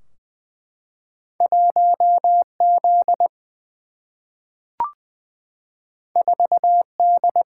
Morse code, heard as a keyed tone, about 20 wpm, 700 Hz. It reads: 1Z 4D
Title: Single-Letter Single-Number 20wpm